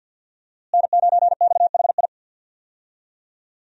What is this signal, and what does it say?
Morse code, heard as a keyed tone, about 47 wpm, 700 Hz. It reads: N9XHI
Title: USA Call Signs - 5 Characters - Repeated 3x - 50wpm